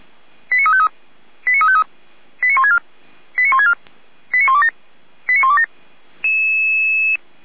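Five-tone selective call (ZVEI), heard as quick runs of five tones, tones of about 70 ms. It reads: 89353 89353 89164 89164 89127 89127
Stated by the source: ZVEI five-tone call 89502 received by SDR, channel 455, from feuerwehr-orlamuende.org fire brigade server.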